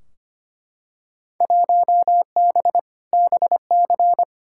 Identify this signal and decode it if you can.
Morse code, heard as a keyed tone, about 25 wpm, 700 Hz. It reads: 1B BC
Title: Field Day Exchanges - 25wpm